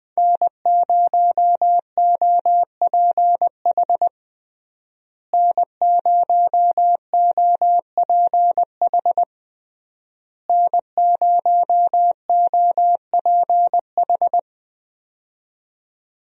Morse code, heard as a keyed tone, about 21 wpm, 700 Hz. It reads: N0OPH N0OPH N0OPH
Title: USA Call Signs - 5 Characters - Repeated 3x - 20wpm